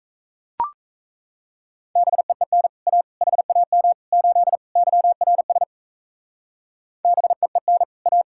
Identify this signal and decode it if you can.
Morse code, heard as a keyed tone, about 37 wpm, 700 Hz. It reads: BEEN A HAM 8 YRS BEEN A